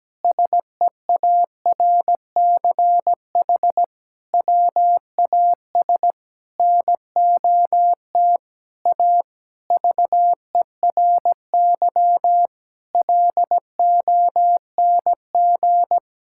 Morse code, heard as a keyed tone, about 17 wpm, 700 Hz. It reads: SEARCH WAS NOT A VERY LONG
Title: Sentences from Top 1000 Words - Review Part 5 - 17wpm